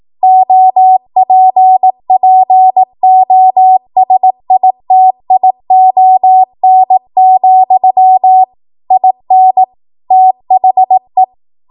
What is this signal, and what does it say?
Morse code, heard as a keyed tone, about 18 wpm, 750 Hz. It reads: OPPOSITION, IN THE